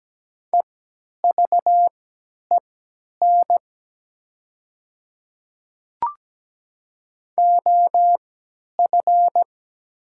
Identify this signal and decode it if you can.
Morse code, heard as a keyed tone, about 17 wpm, spaced as for 10 wpm, 700 Hz. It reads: EVEN OF
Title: 3x Character Spacing - Top 300 Words - 17wpm